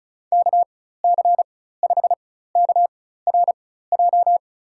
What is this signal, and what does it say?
Morse code, heard as a keyed tone, about 34 wpm, spaced as for 16 wpm, 700 Hz. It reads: KC5KRJ
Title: USA Call Signs - 6 Characters - 4x Character Spacing - 35wpm